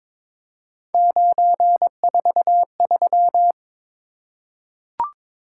Morse code, heard as a keyed tone, about 23 wpm, 700 Hz. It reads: 943
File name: practice / 22wpm / Three Letter-Number Combinations - 22wpm